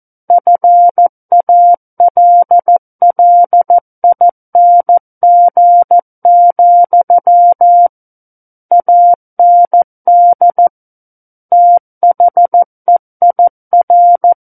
Morse code, heard as a keyed tone, about 14 wpm, 700 Hz. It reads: FALLING, AND THEIR
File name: Book One / 05-Chapter 5 - 14 WPM